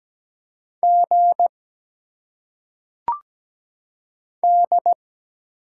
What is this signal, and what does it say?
Morse code, heard as a keyed tone, about 17 wpm, 700 Hz. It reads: G D
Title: Single Letters - Phonetic Pronunciation 17wpm